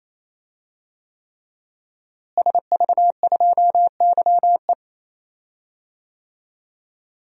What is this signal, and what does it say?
Morse code, heard as a keyed tone, about 28 wpm, 700 Hz. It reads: SV2YE
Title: DXCC-100 Call Signs - 5 Characters - Repeated 3x - 28wpm